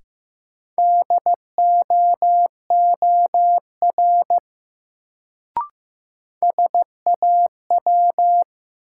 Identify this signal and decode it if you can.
Morse code, heard as a keyed tone, about 15 wpm, 700 Hz. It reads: DOOR SAW
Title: Sentences from Top 300 Words - New Word Drill - 15wpm